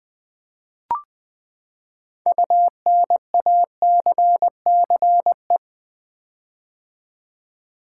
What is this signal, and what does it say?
Morse code, heard as a keyed tone, about 20 wpm, 700 Hz. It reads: UNACCE
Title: Build A Word - 6 Letters At A Time - 12-Letter Words - 20wpm